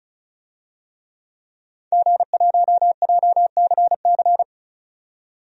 Morse code, heard as a keyed tone, about 36 wpm, 700 Hz. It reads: G1JCC
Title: DXCC-100 Call Signs - 5 Characters - 35wpm